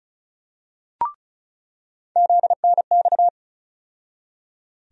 Morse code, heard as a keyed tone, about 36 wpm, 700 Hz. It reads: ZNX
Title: USA Call Sign Postfixes - 3 Letters 35wpm